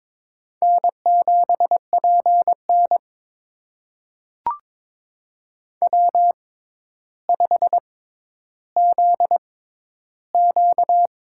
Morse code, heard as a keyed tone, about 22 wpm, spaced as for 11 wpm, 700 Hz. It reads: N7PN W5ZQ
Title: USA Call Signs - 6x Character Spacing - 22wpm